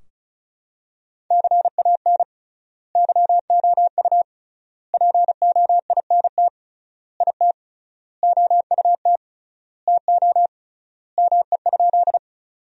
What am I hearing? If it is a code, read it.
Morse code, heard as a keyed tone, about 34 wpm, 700 Hz. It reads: CAN YOU POINT IT OUT TO ME?
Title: 3x Word-Spacing - Sentences from Top 200 Words - 35wpm